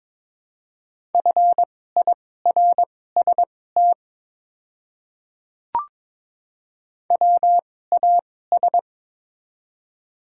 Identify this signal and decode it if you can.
Morse code, heard as a keyed tone, about 22 wpm, spaced as for 16 wpm, 690 Hz. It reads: FIRST WAS
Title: 2x Character Spacing - Top 100 Words - 22wpm